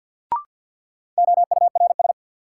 Morse code, heard as a keyed tone, about 48 wpm, 700 Hz. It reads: OURS
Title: Sentences from Top 1000 Words - Review Part 5 - New Word Drill - 50wpm